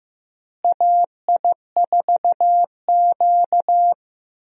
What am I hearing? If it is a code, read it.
Morse code, heard as a keyed tone, about 15 wpm, 700 Hz. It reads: AI4Q
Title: USA Call Signs - 4 Characters - Repeated 2x - 15wpm